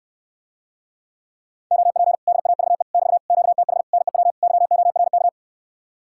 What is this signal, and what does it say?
Morse code, heard as a keyed tone, about 66 wpm, 700 Hz. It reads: OP DAVE <BT> QTH NEW YORK